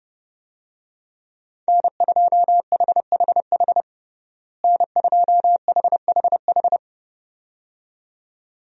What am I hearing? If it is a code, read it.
Morse code, heard as a keyed tone, about 32 wpm, 700 Hz. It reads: N2HHH N2HHH
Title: USA Call Signs - 5 Characters - Repeated 2x - 30wpm